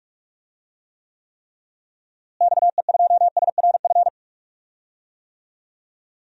Morse code, heard as a keyed tone, about 42 wpm, 700 Hz. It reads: XE2SRF
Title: DXCC-100 Call Signs - 6 Characters - Repeated 3x - 45wpm